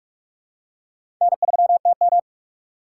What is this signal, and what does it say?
Morse code, heard as a keyed tone, about 46 wpm, 700 Hz. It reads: N3TM
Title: USA Call Signs - 4 Characters - 4x Character Spacing - 45wpm